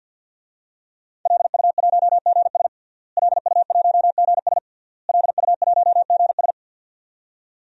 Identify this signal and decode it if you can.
Morse code, heard as a keyed tone, about 51 wpm, 700 Hz. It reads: LU1GS LU1GS LU1GS
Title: DXCC-100 Call Signs - 5 Characters - Repeated 3x - 50wpm